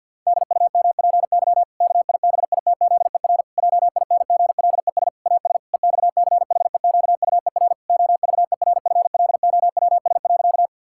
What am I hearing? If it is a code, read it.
Morse code, heard as a keyed tone, about 50 wpm, 700 Hz. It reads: DUMPY KIBITZER JINGLES AS EXCHEQUER OVERFLOWS.
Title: Pangram Challenge v2 - 50wpm